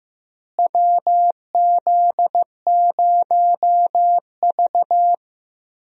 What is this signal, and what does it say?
Morse code, heard as a keyed tone, about 15 wpm, 700 Hz. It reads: WZ0V
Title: USA Call Signs - 4 Characters - Repeated 3x - 15wpm